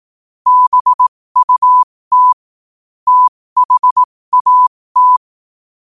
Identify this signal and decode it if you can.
Morse code, heard as a keyed tone, about 18 wpm, spaced as for 15 wpm, 1000 Hz. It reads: BUT THAT
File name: 15wpm